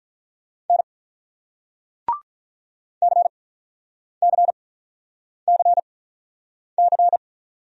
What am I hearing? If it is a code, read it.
Morse code, heard as a keyed tone, about 49 wpm, spaced as for 9 wpm, 700 Hz. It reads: N CCCC